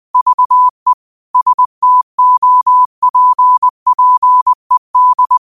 Morse code, heard as a keyed tone, about 20 wpm, 1000 Hz. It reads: VE STOPPED
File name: 20wpm